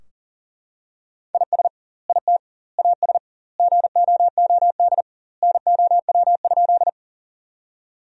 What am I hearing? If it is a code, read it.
Morse code, heard as a keyed tone, about 40 wpm, 700 Hz. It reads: IS IT AS GOOD NOW?
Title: 2x Word-Spacing - Sentences from Top 200 Words - 40wpm